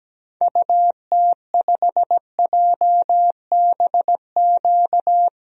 Morse code, heard as a keyed tone, about 17 wpm, 700 Hz. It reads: UT5JBQ
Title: DXCC-100 Call Signs - 6 Characters - 17wpm